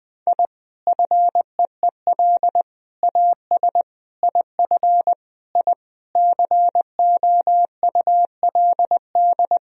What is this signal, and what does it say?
Morse code, heard as a keyed tone, about 20 wpm, 700 Hz. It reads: I FEEL AS IF I COULD
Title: Sentences from Top 1000 Words - Review Part 3 - 20wpm